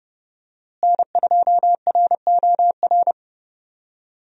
Morse code, heard as a keyed tone, about 30 wpm, 700 Hz. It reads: N2ROR
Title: USA Call Signs - 5 Characters - Repeated 2x - 30wpm